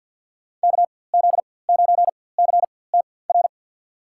Morse code, heard as a keyed tone, about 47 wpm, spaced as for 23 wpm, 700 Hz. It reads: KZ9CTR
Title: USA Call Signs - 6 Characters - 4x Character Spacing - 50wpm